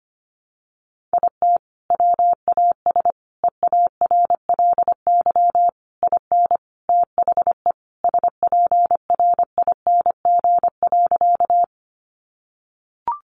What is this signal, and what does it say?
Morse code, heard as a keyed tone, about 25 wpm, 700 Hz. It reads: IT WAS EARLY IN THE SPRING.
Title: Sentences from Top 1200 Words 25wpm